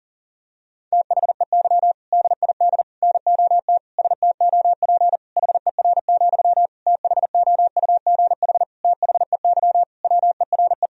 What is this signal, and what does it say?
Morse code, heard as a keyed tone, about 37 wpm, 700 Hz. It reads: THEY DID NOT STOP HER, THOUGH THEY WERE
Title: Sentences from Top 1000 Words - Review Part 2 - 40wpm